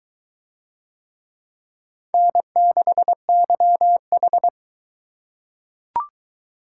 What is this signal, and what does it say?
Morse code, heard as a keyed tone, about 22 wpm, 700 Hz. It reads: N6YH